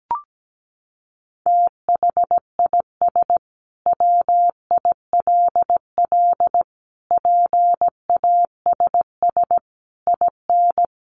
Morse code, heard as a keyed tone, about 17 wpm, 700 Hz. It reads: THIS WILL PASS IN
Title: Sentences from Top 400 Words 17wpm